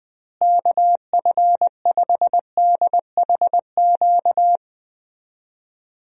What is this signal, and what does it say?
Morse code, heard as a keyed tone, about 20 wpm, 700 Hz. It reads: KF5DHQ
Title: USA Call Signs - 6 Characters - Repeated 3x - 20wpm